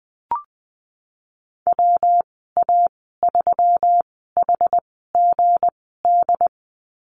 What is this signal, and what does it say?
Morse code, heard as a keyed tone, about 19 wpm, spaced as for 14 wpm, 700 Hz. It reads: WA3HGD
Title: 179 Lesson 29g - Slash - Call Signs 20wpm